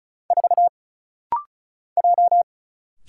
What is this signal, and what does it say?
Morse code, heard as a keyed tone, about 34 wpm, 700 Hz. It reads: 4 J
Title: Single Letter-Number - Rapid-Fire 35wpm